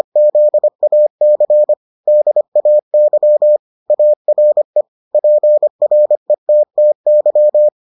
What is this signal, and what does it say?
Morse code, heard as a keyed tone, about 25 wpm, 600 Hz. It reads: ZAC DAY ARE PRETTY